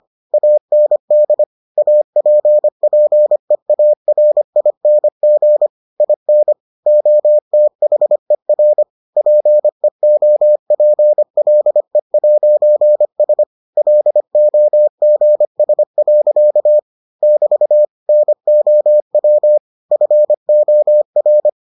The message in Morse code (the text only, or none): AND APPEARING IN OTHER PEOPLE'S LOGS. <BT> NOW FOR